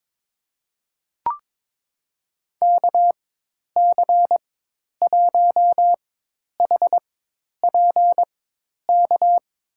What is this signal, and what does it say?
Morse code, heard as a keyed tone, about 21 wpm, spaced as for 10 wpm, 700 Hz. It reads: KC1HPK
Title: USA Call Signs - 6 Characters - 4x Character Spacing - 22wpm